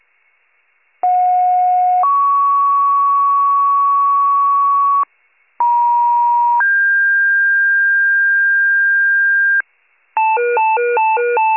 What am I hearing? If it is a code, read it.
Two-tone page, heard as a one-second tone, then a three-second tone, about 730 then 1120 Hz, about 950 then 1600 Hz, with noise. It is heard from 1.0 to 9.6 s.